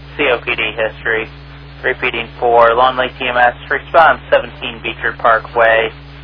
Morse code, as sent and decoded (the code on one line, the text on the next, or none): none